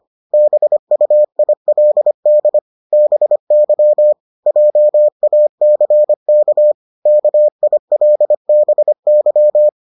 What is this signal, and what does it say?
Morse code, heard as a keyed tone, about 25 wpm, 600 Hz. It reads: BUILD BY JACK KILBY